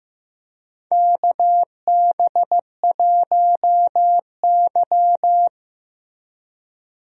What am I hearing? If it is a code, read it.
Morse code, heard as a keyed tone, about 16 wpm, 700 Hz. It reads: KB1Y
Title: USA Call Signs - 4 Characters - Repeated 3x - 15wpm